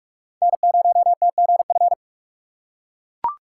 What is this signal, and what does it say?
Morse code, heard as a keyed tone, about 42 wpm, 700 Hz. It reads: N0TGF
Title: USA Call Signs - 5 Characters - 8x Character Spacing - 45wpm